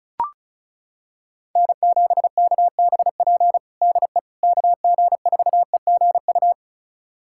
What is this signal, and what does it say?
Morse code, heard as a keyed tone, about 34 wpm, 700 Hz. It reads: N7KBP DE KG4EGU
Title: USA Callsign DE Callsign 35wpm